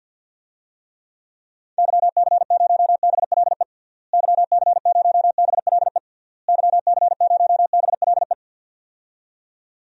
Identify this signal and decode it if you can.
Morse code, heard as a keyed tone, about 52 wpm, 700 Hz. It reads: YC0BLE YC0BLE YC0BLE
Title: DXCC-100 Call Signs - 6 Characters - Repeated 3x - 50wpm